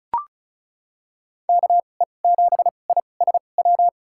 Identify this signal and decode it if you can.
Morse code, heard as a keyed tone, about 34 wpm, spaced as for 25 wpm, 700 Hz. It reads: KE7ISW
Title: USA Call Signs - 6 Characters - 2x Character Spacing - 35wpm